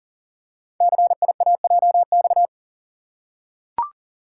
Morse code, heard as a keyed tone, about 42 wpm, 700 Hz. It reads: CIAJX
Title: Code Groups - 5 Letters - 40wpm